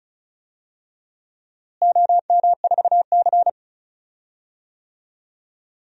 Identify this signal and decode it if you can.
Morse code, heard as a keyed tone, about 36 wpm, 700 Hz. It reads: OM4C
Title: DXCC-100 Call Signs - 4 Characters - Repeated 3x - 35wpm